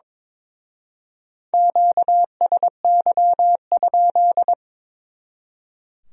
Morse code, heard as a keyed tone, about 26 wpm, 700 Hz. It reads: QSY?
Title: Speed-Racing Q-Codes - 22wpm